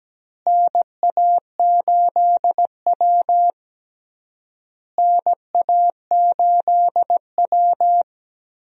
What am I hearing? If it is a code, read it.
Morse code, heard as a keyed tone, about 17 wpm, 700 Hz. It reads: NA8W NA8W
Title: USA Call Signs - 4 Characters - Repeated 3x - 17wpm